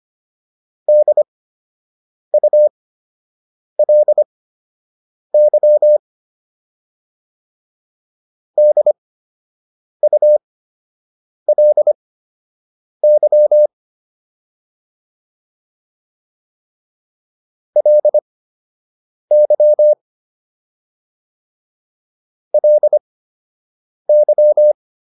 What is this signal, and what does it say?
Morse code, heard as a keyed tone, about 25 wpm, spaced as for 7 wpm, 600 Hz. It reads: DULY DULY LY LY
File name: LY4 07